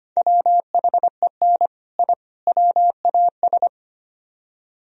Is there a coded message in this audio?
Yes: Morse code from 0.2 to 3.7 s, heard as a keyed tone, about 32 wpm, 700 Hz. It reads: WHEN I WAS